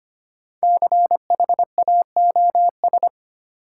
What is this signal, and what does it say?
Morse code, heard as a keyed tone, about 25 wpm, 700 Hz. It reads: CHAOS
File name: 5 Letter Words - First 3 Letters Given - 25wpm